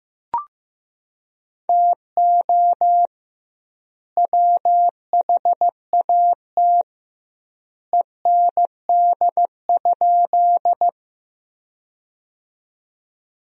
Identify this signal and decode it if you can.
Morse code, heard as a keyed tone, about 15 wpm, 700 Hz. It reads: TO WHAT END?
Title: 2x Word-Spacing - Sentences from Top 200 Words - 15wpm